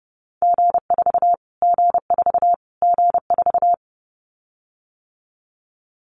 Morse code, heard as a keyed tone, about 30 wpm, 700 Hz. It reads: G4 G4 G4
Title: Callsign Prefixes for Top 25 DXCC - Repeated 3x - 30wpm